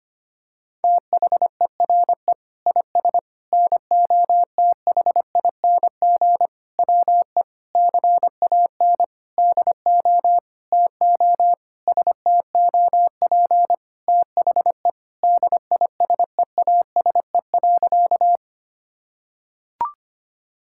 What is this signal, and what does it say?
Morse code, heard as a keyed tone, about 25 wpm, 700 Hz. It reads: THERE IS NOTHING WE CAN DO TO STOP THE DISEASE.
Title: Sentences from Top 1000 Words - Review Part 6 - 25wpm